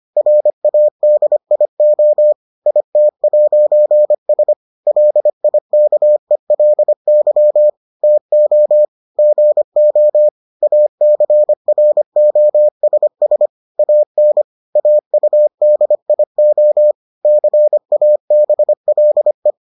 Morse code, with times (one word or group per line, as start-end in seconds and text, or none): 0.2-2.3 RADIO
2.7-4.5 IT'S
4.9-7.7 LIKELY
8.0-8.9 TO
9.2-10.3 GO
10.6-13.5 ACROSS
13.8-14.4 AN
14.8-16.9 AUDIO
17.2-19.5 CABLE